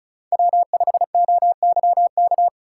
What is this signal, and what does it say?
Morse code, heard as a keyed tone, about 34 wpm, 700 Hz. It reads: W5OYK
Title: USA Call Signs - 5 Characters - 4x Character Spacing - 35wpm